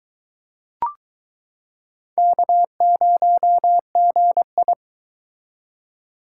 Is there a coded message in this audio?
Yes: Morse code from 2.2 to 4.7 s, heard as a keyed tone, about 22 wpm, 700 Hz. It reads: K0GI